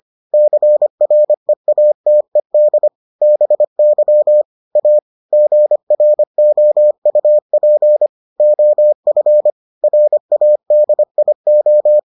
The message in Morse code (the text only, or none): CREATED BY A GROUP OF RADIO